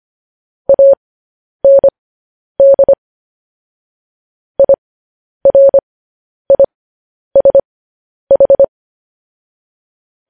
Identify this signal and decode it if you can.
Morse code, heard as a keyed tone, about 25 wpm, spaced as for 10 wpm, 550 Hz. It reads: AND IRISH
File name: podcast / News headlines-1x-25x10